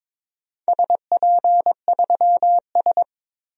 Morse code, heard as a keyed tone, about 22 wpm, 700 Hz. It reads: SP3S
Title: DXCC-100 Call Signs - 4 Characters - 22wpm